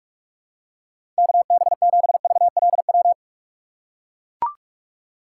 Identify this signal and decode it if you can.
Morse code, heard as a keyed tone, about 41 wpm, 700 Hz. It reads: KB7VLW
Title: USA Call Signs - 6 Characters - 3x Character Spacing - 45wpm